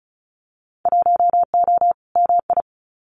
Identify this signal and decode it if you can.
Morse code, heard as a keyed tone, about 36 wpm, 700 Hz. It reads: 1O MI
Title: Winter Field Day Exchanges 35wpm